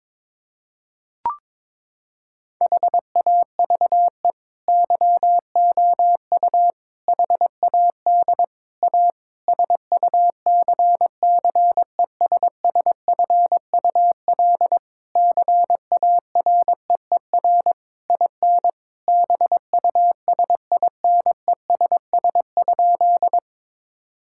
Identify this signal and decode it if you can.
Morse code, heard as a keyed tone, about 22 wpm, 700 Hz. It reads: HAVE YOU HAD A SUCCESSFUL CAREER IN BUSINESS?
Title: Sentences from Top 1000 Words - Review Part 6 - 22wpm